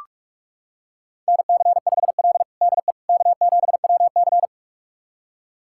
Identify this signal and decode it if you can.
Morse code, heard as a keyed tone, about 46 wpm, 700 Hz. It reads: NC5L DE K7WC
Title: USA Callsign DE Callsign - 4 Characters - 45wpm